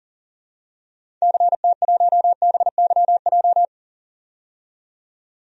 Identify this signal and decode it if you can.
Morse code, heard as a keyed tone, about 40 wpm, 700 Hz. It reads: CT1BYJ
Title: DXCC-100 Call Signs - 6 Characters - 40wpm